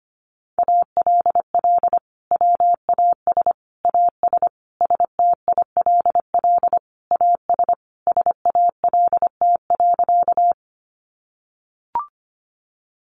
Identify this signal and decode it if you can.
Morse code, heard as a keyed tone, about 25 wpm, 700 Hz. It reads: ALL WAS AS STILL AS SALT.
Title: Sentences from Top 800 Words 25wpm